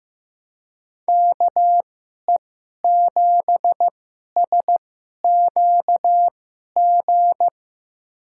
Morse code, heard as a keyed tone, about 15 wpm, spaced as for 11 wpm, 700 Hz. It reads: KE7SQG
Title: USA Call Signs - 6 Characters - 2x Character Spacing - 15wpm